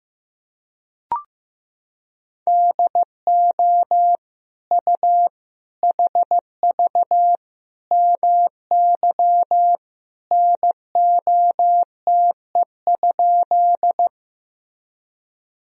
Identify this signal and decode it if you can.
Morse code, heard as a keyed tone, about 15 wpm, 700 Hz. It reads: DO U HV MY NOTE?
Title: Sentences from Top 500 Words and Abbreviations 15wpm